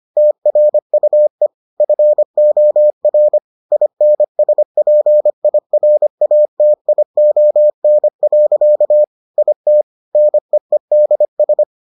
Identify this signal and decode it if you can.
Morse code, heard as a keyed tone, about 25 wpm, 600 Hz. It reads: TRUE FOR INSPIRATION. IT NEEDS